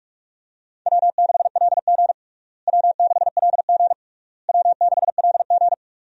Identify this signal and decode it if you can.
Morse code, heard as a keyed tone, about 42 wpm, 700 Hz. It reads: W6LG W6LG W6LG
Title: USA Call Signs - 4 Characters - Repeated 3x - 45wpm